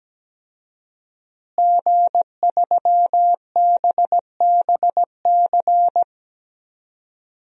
Morse code, heard as a keyed tone, about 17 wpm, 700 Hz. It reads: G3BBC
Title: DXCC-100 Call Signs - 5 Characters - Repeated 3x - 17wpm